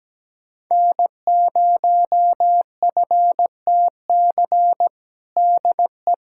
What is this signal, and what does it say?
Morse code, heard as a keyed tone, about 17 wpm, 700 Hz. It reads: N0FTC DE
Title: USA Callsign DE Callsign - 5 Characters - 17wpm